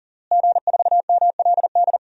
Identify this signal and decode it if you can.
Morse code, heard as a keyed tone, about 40 wpm, 700 Hz. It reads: G4MLD
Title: DXCC-100 Call Signs - 5 Characters - Repeated 3x - 40wpm